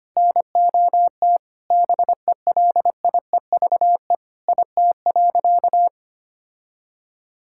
Morse code, heard as a keyed tone, about 25 wpm, 700 Hz. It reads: NOT BELIEVE IT.
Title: Sentences from Top 1000 Words - Review Part 6 - 25wpm